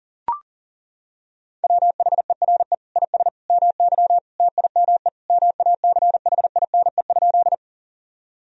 Morse code, heard as a keyed tone, about 37 wpm, 700 Hz. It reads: WHERE IS MY TIME MACHINE?
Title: Sentences from Top 1000 Words - Review Part 2 - 40wpm